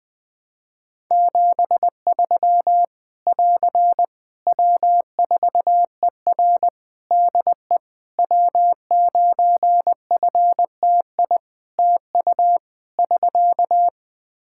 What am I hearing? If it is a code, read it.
Morse code, heard as a keyed tone, about 20 wpm, 700 Hz. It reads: 73 <AR> W4ER DE W9FTI TU <SK>
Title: Practice QSO - Exchange 4 - Ending - 20wpm